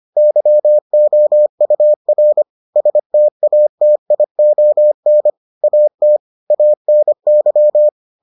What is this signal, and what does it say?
Morse code, heard as a keyed tone, about 25 wpm, 600 Hz. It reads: YOUR STATION AT ANY